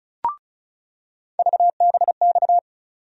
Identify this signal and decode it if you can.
Morse code, heard as a keyed tone, about 36 wpm, 700 Hz. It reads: VBX
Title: USA Call Sign Postfixes - 3 Letters 35wpm